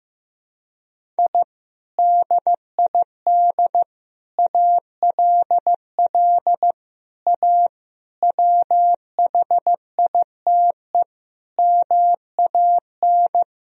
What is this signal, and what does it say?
Morse code, heard as a keyed tone, about 15 wpm, 700 Hz. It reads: I DID ALL A WHITE MAN
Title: Sentences from Top 1000 Words - Review Part 1 - 15wpm